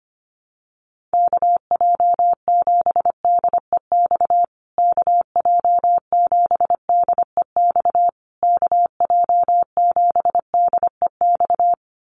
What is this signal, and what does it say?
Morse code, heard as a keyed tone, about 25 wpm, 700 Hz. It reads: KJ7DEX KJ7DEX KJ7DEX
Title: USA Call Signs Repeated 3x 25wpm